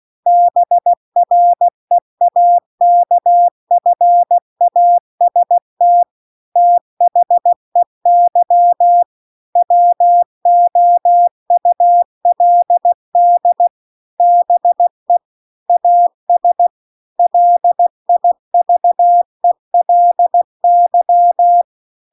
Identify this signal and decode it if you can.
Morse code, heard as a keyed tone, about 16 wpm, 700 Hz. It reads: BREAKFAST THEY WOULD BE AS LIVELY
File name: 16WPM